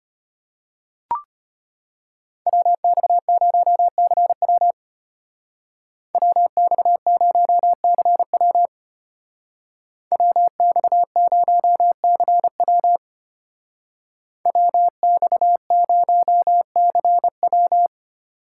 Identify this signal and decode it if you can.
Morse code, heard as a keyed tone, about 34 wpm, 700 Hz. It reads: WX0CW WX0CW WX0CW WX0CW